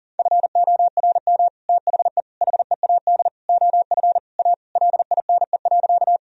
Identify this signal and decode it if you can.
Morse code, heard as a keyed tone, about 40 wpm, 700 Hz. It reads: FORM THE HEAD OF A LINE.
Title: Sentences from Top 1000 Words - Review Part 1 - 40wpm